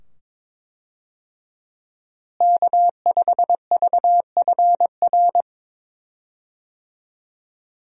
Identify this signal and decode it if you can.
Morse code, heard as a keyed tone, about 22 wpm, 700 Hz. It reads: K5VFR
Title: USA Call Signs - 5 Characters - Repeated 2x - 22wpm